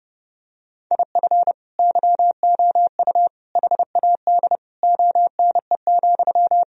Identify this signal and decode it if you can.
Morse code, heard as a keyed tone, about 30 wpm, 700 Hz. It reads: IF YOU HAD ONE,
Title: Sentences from Top 1000 Words - Review Part 3 - 30wpm